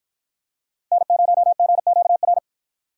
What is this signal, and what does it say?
Morse code, heard as a keyed tone, about 53 wpm, 700 Hz. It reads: N0GQR